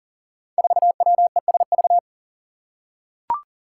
Morse code, heard as a keyed tone, about 42 wpm, 700 Hz. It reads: 4WESV
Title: Five Challenging Letter-Number Combinations 40wpm